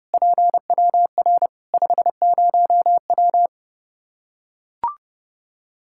Morse code, heard as a keyed tone, about 30 wpm, 700 Hz. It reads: PWR 50W
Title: Practice QSO - Exchange 2 - Info - 30wpm